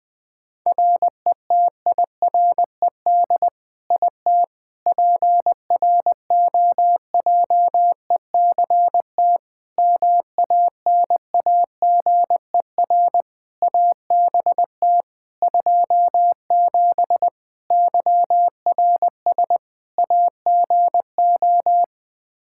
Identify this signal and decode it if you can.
Morse code, heard as a keyed tone, about 20 wpm, 700 Hz. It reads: RETIRED IT PROJECT MANAGER ABT 27 YRS AGO
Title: Practice QSO - Exchange 3 - More Info - 20wpm